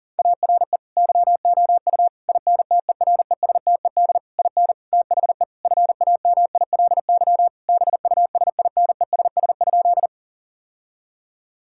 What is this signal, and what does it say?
Morse code, heard as a keyed tone, about 40 wpm, 700 Hz. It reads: ARE YOU INTERESTED IN THE FAMILY BUSINESS?
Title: Sentences from Top 1000 Words - Review Part 6 - 40wpm